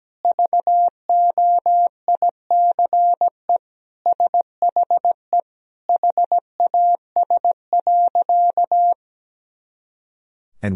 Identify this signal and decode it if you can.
Morse code, heard as a keyed tone, about 17 wpm, 700 Hz. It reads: VOICE SHE HAS.
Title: Sentences from Top 500 Words and Abbreviations 17wpm